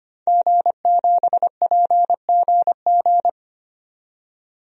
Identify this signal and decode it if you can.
Morse code, heard as a keyed tone, about 25 wpm, 700 Hz. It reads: G7PGG
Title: DXCC-100 Call Signs - 5 Characters - 25wpm